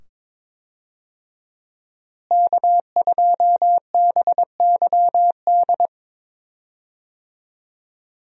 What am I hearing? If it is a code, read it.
Morse code, heard as a keyed tone, about 21 wpm, 700 Hz. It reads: K2BYD
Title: USA Call Signs - 5 Characters - Repeated 2x - 22wpm